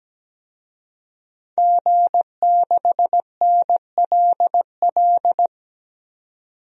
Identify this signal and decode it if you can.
Morse code, heard as a keyed tone, about 17 wpm, 700 Hz. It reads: G6NLL
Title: DXCC-100 Call Signs - 5 Characters - 17wpm